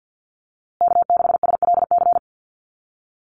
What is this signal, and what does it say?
Morse code, heard as a keyed tone, about 50 wpm, 700 Hz. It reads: K6SLC
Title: USA Call Signs Repeated 2x 50wpm